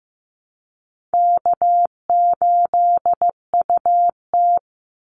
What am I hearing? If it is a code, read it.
Morse code, heard as a keyed tone, about 15 wpm, 700 Hz. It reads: K8UT
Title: Easy Call Signs Repeated 3x - 15wpm